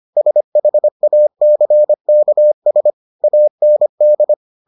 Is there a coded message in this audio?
Yes: Morse code from 0.2 to 4.3 s, heard as a keyed tone, about 25 wpm, 600 Hz. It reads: SHACKS AND